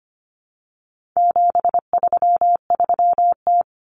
Morse code, heard as a keyed tone, about 25 wpm, 700 Hz. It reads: 733T
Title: Common Cut Numbers - 4-Digit 25wpm